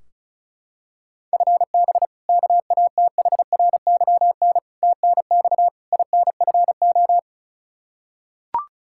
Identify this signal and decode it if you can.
Morse code, heard as a keyed tone, about 36 wpm, 700 Hz. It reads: FB KATHRYN TNX INFO
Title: Practice QSO - Exchange 3 - More Info - 35wpm